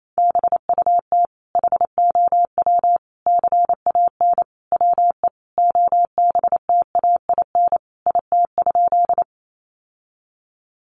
Morse code, heard as a keyed tone, about 28 wpm, 700 Hz. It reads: BUT HOW CAN WE OBTAIN IT?
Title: Sentences from Top 1300 Words 28wpm